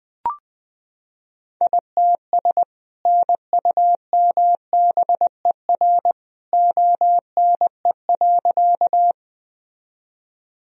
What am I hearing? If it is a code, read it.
Morse code, heard as a keyed tone, about 21 wpm, 700 Hz. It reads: ITS NUMBER ONE.
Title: Sentences from Top 1000 Words - Review Part 1 - 20wpm